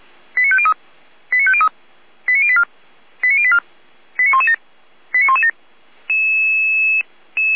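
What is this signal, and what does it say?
Five-tone selective call (ZVEI), heard as quick runs of five tones, tones of about 70 ms. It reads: 89503 89503 89084 89084 89117 89117
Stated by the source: ZVEI five-tone call received on SDR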